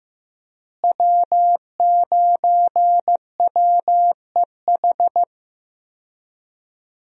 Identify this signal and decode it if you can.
Morse code, heard as a keyed tone, about 15 wpm, 700 Hz. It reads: W9WEH
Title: USA Call Signs - 5 Characters - Repeated 2x - 15wpm